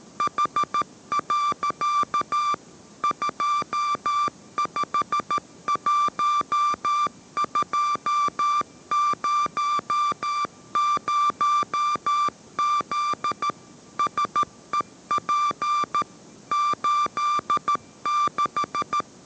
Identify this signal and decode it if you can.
Morse code, heard as a keyed tone, about 13 wpm, 1180 Hz. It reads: H. 251200Z SEP 86
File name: 14wpm